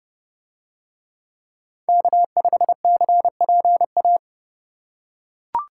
Morse code, heard as a keyed tone, about 30 wpm, 700 Hz. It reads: K5CPA